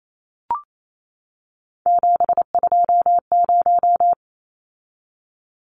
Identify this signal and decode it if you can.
Morse code, heard as a keyed tone, about 28 wpm, 700 Hz. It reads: 720
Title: Numbers - 3-Digit 28wpm